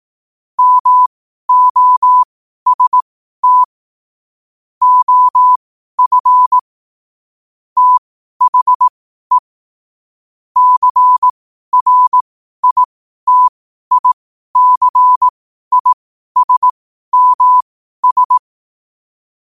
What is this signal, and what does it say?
Morse code, heard as a keyed tone, about 18 wpm, spaced as for 12 wpm, 1000 Hz. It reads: MOST OF THE CRITICISMS
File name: qotdfiles / 12wpm